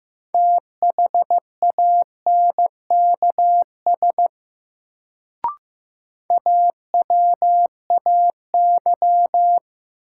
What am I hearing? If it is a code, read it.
Morse code, heard as a keyed tone, about 15 wpm, 700 Hz. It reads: THANKS AWAY
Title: Sentences from Top 600 Words - New Word Drill - 15wpm